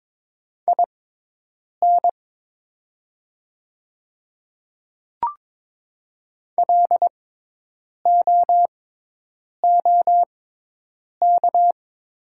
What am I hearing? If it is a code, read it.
Morse code, heard as a keyed tone, about 22 wpm, spaced as for 8 wpm, 700 Hz. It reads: IN LOOK
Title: 6x Character Spacing - Top 100 Words - 22wpm